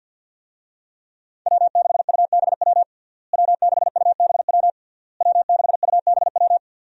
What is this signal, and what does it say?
Morse code, heard as a keyed tone, about 51 wpm, 700 Hz. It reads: W6UBW W6UBW W6UBW
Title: USA Call Signs - 5 Characters - Repeated 3x - 50wpm